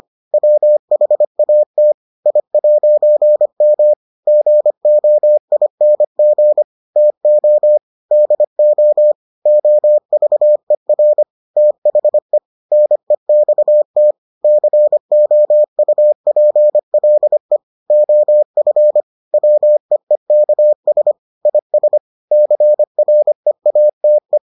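Morse code, heard as a keyed tone, about 25 wpm, 600 Hz. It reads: WHAT I'M GOING TO DO OVER THE NEXT COUPLE OF WEEKS IS CREATE
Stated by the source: Morse edition of the Foundations of Amateur Radio episodes.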